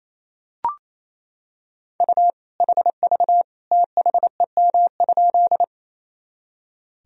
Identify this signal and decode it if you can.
Morse code, heard as a keyed tone, about 28 wpm, 700 Hz. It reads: U HV THEM?
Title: Sentences from Top 100 Words and Abbreviations 28wpm